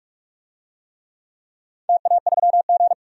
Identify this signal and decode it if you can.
Morse code, heard as a keyed tone, about 46 wpm, 700 Hz. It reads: TA3G
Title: DXCC-100 Call Signs - 4 Characters - 45wpm